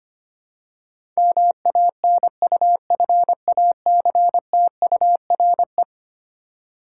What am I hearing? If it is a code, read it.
Morse code, heard as a keyed tone, about 25 wpm, 700 Hz. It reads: MANUFACTURE
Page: morse-code-ninja-practice-25wpm